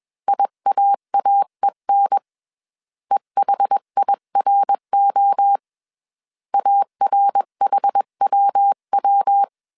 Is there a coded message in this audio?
Yes: Morse code from 0.3 to 9.4 s, heard as a keyed tone, about 21 wpm, 800 Hz. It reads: IAAEN EHIRO ARHWW